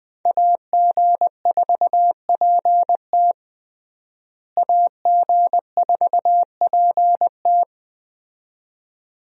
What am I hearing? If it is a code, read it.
Morse code, heard as a keyed tone, about 21 wpm, 700 Hz. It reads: AG4PT AG4PT
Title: USA Call Signs - 5 Characters - Repeated 2x - 20wpm